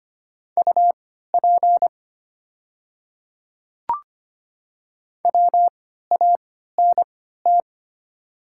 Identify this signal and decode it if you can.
Morse code, heard as a keyed tone, about 25 wpm, spaced as for 14 wpm, 700 Hz. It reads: UP WANT
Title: 3x Character Spacing - Top 200 Words - 25wpm